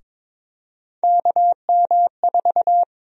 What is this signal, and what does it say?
Morse code, heard as a keyed tone, about 23 wpm, 700 Hz. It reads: KM4
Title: USA Call Sign Prefixes - 22wpm